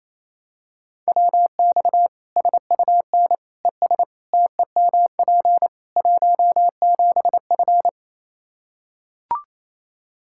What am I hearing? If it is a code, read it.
Morse code, heard as a keyed tone, about 28 wpm, 700 Hz. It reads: WX SUN ES TEMP 17F
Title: Practice QSO - Exchange 2 - Info - 28wpm